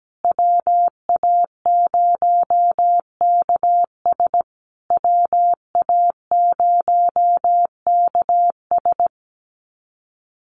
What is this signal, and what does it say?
Morse code, heard as a keyed tone, about 17 wpm, 700 Hz. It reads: WA0KS WA0KS
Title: USA Call Signs Repeated 3x 17wpm